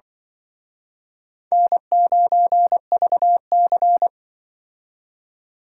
Morse code, heard as a keyed tone, about 25 wpm, 700 Hz. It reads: N9VC
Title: Speed Racing - USA 1x2 Call Signs 20wpm